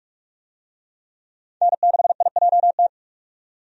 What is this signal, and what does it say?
Morse code, heard as a keyed tone, about 46 wpm, 700 Hz. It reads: N6IJT
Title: USA Call Signs - 5 Characters - 45wpm